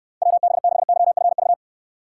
Morse code, heard as a keyed tone, about 69 wpm, 700 Hz. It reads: WB6YFV